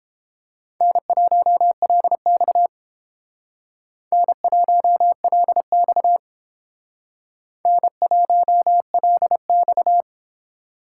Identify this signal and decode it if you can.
Morse code, heard as a keyed tone, about 28 wpm, 700 Hz. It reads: N1LX N1LX N1LX